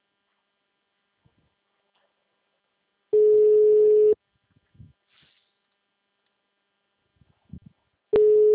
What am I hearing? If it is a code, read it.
Ringback tone (Europe, 425 Hz).
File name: out- -2005- - - .123